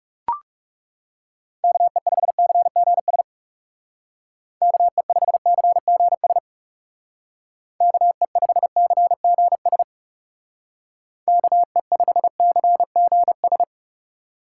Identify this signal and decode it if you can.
Morse code, heard as a keyed tone, about 40 wpm, 700 Hz. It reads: KE5CGS KE5CGS KE5CGS KE5CGS